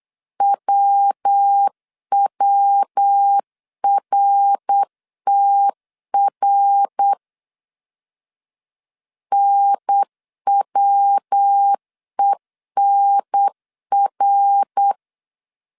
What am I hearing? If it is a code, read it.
Morse code, heard as a keyed tone, about 8 wpm, 800 Hz. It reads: WWRTR NWENR
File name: CW / DET / 1 3 1500C S B 8wpm 4141